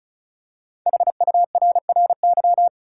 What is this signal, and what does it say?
Morse code, heard as a keyed tone, about 36 wpm, 700 Hz. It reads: HURRY
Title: Five Letter Words - 35wpm